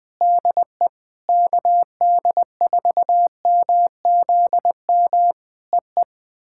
Morse code, heard as a keyed tone, about 21 wpm, 700 Hz. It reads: DE KD4MZM EE
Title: POTA QSO Practice - 20wpm